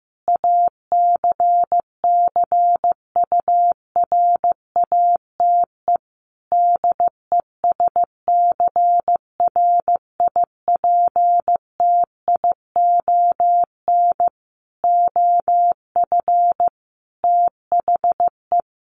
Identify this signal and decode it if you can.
Morse code, heard as a keyed tone, about 15 wpm, 700 Hz. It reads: ACCURATE DESCRIPTION OF THE
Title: Sentences from Top 1300 Words 15wpm